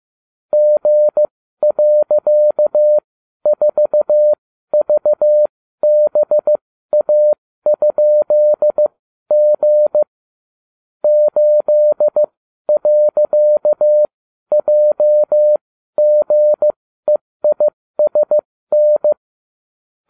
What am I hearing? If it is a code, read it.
Morse code, heard as a keyed tone, about 15 wpm, spaced as for 12 wpm, 600 Hz. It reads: G.4VBA?G 8.JGEISN